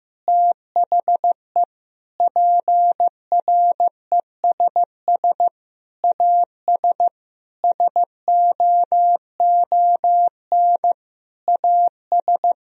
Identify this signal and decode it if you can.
Morse code, heard as a keyed tone, about 15 wpm, 700 Hz. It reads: THE PRESS AS SOON AS
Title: Sentences from Top 1000 Words - Review Part 1 - 15wpm